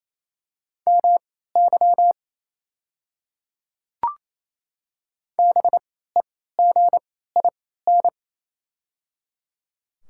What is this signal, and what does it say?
Morse code, heard as a keyed tone, about 28 wpm, spaced as for 16 wpm, 700 Hz. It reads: MY BEGIN
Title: 3x Character Spacing - Top 300 Words - 28wpm